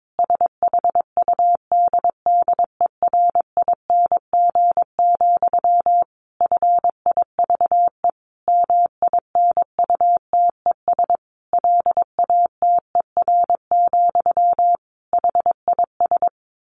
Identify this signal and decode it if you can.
Morse code, heard as a keyed tone, about 22 wpm, 700 Hz. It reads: SHUDDERING, FIVE MINUTES LATER, HIS
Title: A Sound Of Thunder 22wpm